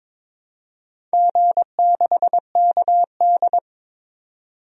Morse code, heard as a keyed tone, about 22 wpm, 700 Hz. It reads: G6KD
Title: DXCC-100 Call Signs - 4 Characters - 22wpm